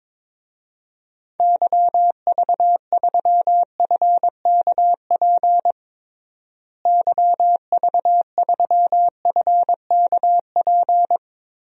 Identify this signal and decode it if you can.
Morse code, heard as a keyed tone, about 22 wpm, 700 Hz. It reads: YV3FKP YV3FKP
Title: DXCC-100 Call Signs - 6 Characters - Repeated 2x - 22wpm